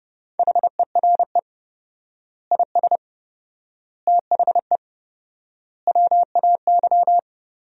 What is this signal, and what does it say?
Morse code, heard as a keyed tone, about 30 wpm, 700 Hz. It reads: HERE IS THE WAY
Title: 4x Word-Spacing - Sentences from Top 200 Words - 30wpm